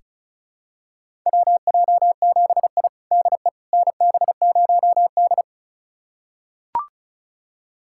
Morse code, heard as a keyed tone, about 36 wpm, 700 Hz. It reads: WJ7I DE NB0D
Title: USA Callsign DE Callsign - 4 Characters - 35wpm